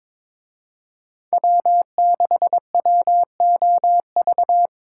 Morse code, heard as a keyed tone, about 26 wpm, 700 Hz. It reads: W6WOV
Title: Speed Racing - USA 1x3 Call Signs 22wpm